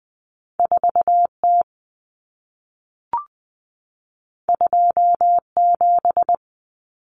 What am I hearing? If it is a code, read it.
Morse code, heard as a keyed tone, about 20 wpm, 700 Hz. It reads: 4T 27
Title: Common Cut Numbers - 2-Digit 20wpm